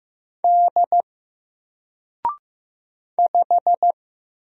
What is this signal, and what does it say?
Morse code, heard as a keyed tone, about 15 wpm, 700 Hz. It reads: D 5
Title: Single Letter-Number Alt 15wpm